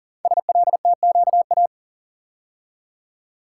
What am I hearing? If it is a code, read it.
Morse code, heard as a keyed tone, about 42 wpm, 690 Hz. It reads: SLTQA
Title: Code Groups - 5 Letters - 40wpm